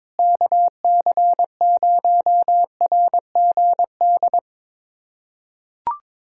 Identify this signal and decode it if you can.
Morse code, heard as a keyed tone, about 22 wpm, 700 Hz. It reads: KC0RGD
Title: USA Callsign DE Callsign - 6 Characters - 22wpm